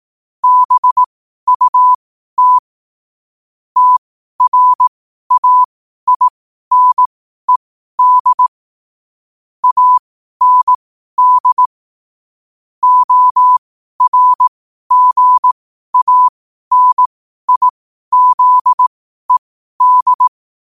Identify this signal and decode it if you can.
Morse code, heard as a keyed tone, about 18 wpm, spaced as for 12 wpm, 1000 Hz. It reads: BUT TRAINED AND ORGANIZED
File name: qotdfiles / 12wpm